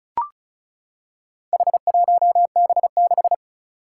Morse code, heard as a keyed tone, about 36 wpm, 690 Hz. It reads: H1B6
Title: Four Challenging Letter-Number Combinations 35wpm